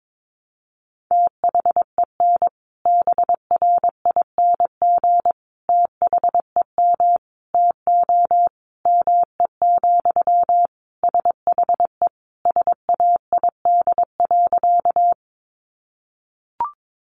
Morse code, heard as a keyed tone, about 22 wpm, 700 Hz. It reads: THEN BRING THEM TO ME, SHE SAID.
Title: Sentences from Top 500 Words 22wpm